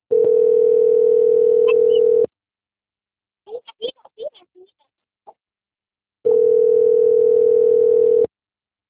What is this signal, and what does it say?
Ringback tone (North America), 440 + 480 Hz.
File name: out- - - - - .56176